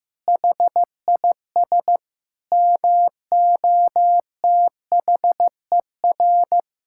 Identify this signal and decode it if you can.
Morse code, heard as a keyed tone, about 15 wpm, 700 Hz. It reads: HIS MOTHER
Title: Sentences from Top 1000 Words - Review Part 5 - 15wpm